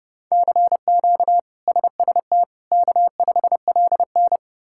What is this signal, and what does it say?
Morse code, heard as a keyed tone, about 32 wpm, 700 Hz. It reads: CQ SST K5LN
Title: K1USN SST QSO Practice - 30wpm